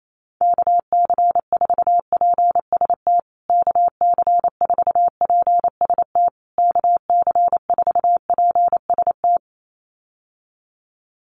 Morse code, heard as a keyed tone, about 28 wpm, 700 Hz. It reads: KC4PST KC4PST KC4PST
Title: USA Call Signs Repeated 3x 28wpm